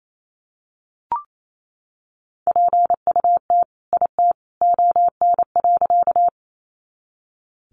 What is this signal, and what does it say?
Morse code, heard as a keyed tone, about 28 wpm, 700 Hz. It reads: PUT IT ON.
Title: Sentences from Top 200 Words 28wpm